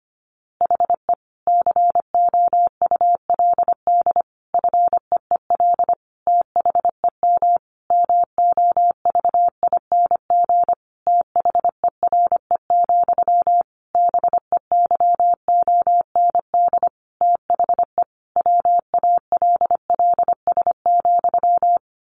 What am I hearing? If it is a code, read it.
Morse code, heard as a keyed tone, about 25 wpm, 700 Hz. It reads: HE COULD FEEL THEM MOVING THERE, BEYOND THE WALLS,
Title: A Sound Of Thunder 25wpm